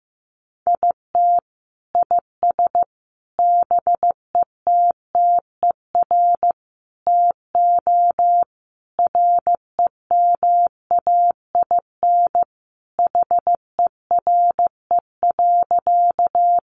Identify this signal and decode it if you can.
Morse code, heard as a keyed tone, about 15 wpm, 700 Hz. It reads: IT IS BETTER TO REMAIN HERE.
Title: Sentences from Top 1200 Words 15wpm